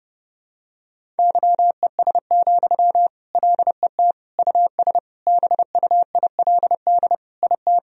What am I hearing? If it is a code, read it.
Morse code, heard as a keyed tone, about 30 wpm, 700 Hz. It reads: YES, LET US BUILD IT
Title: Sentences from Top 1000 Words - Review Part 1 - 30wpm